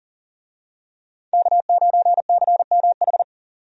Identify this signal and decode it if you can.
Morse code, heard as a keyed tone, about 40 wpm, 700 Hz. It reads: K9CMH
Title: USA Call Signs - 5 Characters - 40wpm